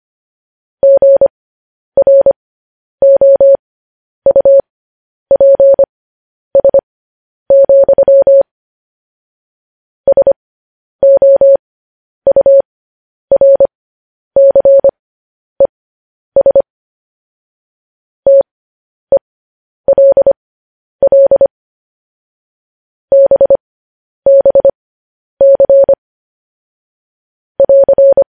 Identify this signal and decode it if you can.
Morse code, heard as a keyed tone, about 25 wpm, spaced as for 10 wpm, 550 Hz. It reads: GROUPS, SOURCES TELL BBC <AR>
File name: podcast / News headlines-1x-25x10